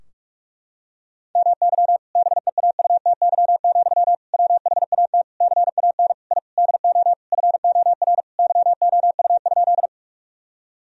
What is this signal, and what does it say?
Morse code, heard as a keyed tone, about 42 wpm, 700 Hz. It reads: MY BEAUTY, WHAT CAN I DO FOR YOU?
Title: Sentences from Top 1000 Words - Review Part 2 - 45wpm